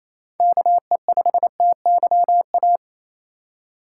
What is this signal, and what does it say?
Morse code, heard as a keyed tone, about 28 wpm, 700 Hz. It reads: KE5TYA
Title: USA Call Signs - 6 Characters - 3x Character Spacing - 28wpm